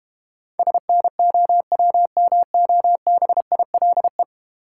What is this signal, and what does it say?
Morse code, heard as a keyed tone, about 32 wpm, 700 Hz. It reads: SNOWMOBILE